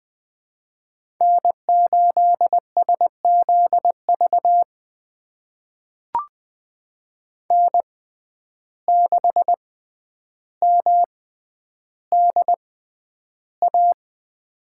Morse code, heard as a keyed tone, about 20 wpm, spaced as for 10 wpm, 700 Hz. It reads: N8SZV N6MDA
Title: USA Call Signs - 6x Character Spacing - 20wpm